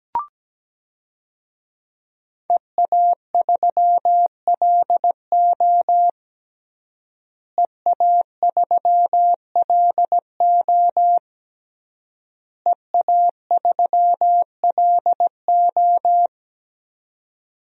Morse code, heard as a keyed tone, about 17 wpm, 700 Hz. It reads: EA3LO EA3LO EA3LO
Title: DXCC-100 Call Signs - 5 Characters - Repeated 3x - 17wpm